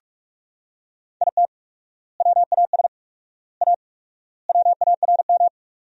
Morse code, heard as a keyed tone, about 41 wpm, 700 Hz. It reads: IT WAS A WARM
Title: 4x Word-Spacing - Sentences from Top 500 Words - 45wpm